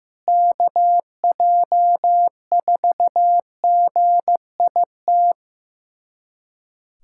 Morse code, heard as a keyed tone, about 15 wpm, 700 Hz. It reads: KJ4GIT
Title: USA Callsign DE Callsign - 6 Characters - 15wpm